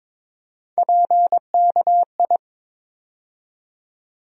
Morse code, heard as a keyed tone, about 22 wpm, 700 Hz. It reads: PKI
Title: USA Call Sign Postfixes - 3 Letters 22wpm